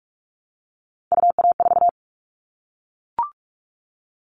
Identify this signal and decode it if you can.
Morse code, heard as a keyed tone, about 44 wpm, 700 Hz. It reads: UA4
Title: Callsign Prefixes for Top 25 DXCC - 45wpm